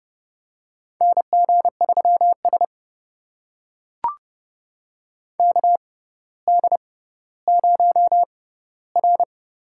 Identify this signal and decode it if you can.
Morse code, heard as a keyed tone, about 30 wpm, spaced as for 15 wpm, 700 Hz. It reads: NG3S KD0R
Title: USA Call Signs - 6x Character Spacing - 30wpm